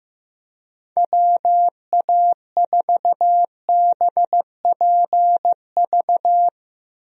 Morse code, heard as a keyed tone, about 15 wpm, 700 Hz. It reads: WA4BPV
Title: USA Call Signs - 6 Characters - Repeated 3x - 15wpm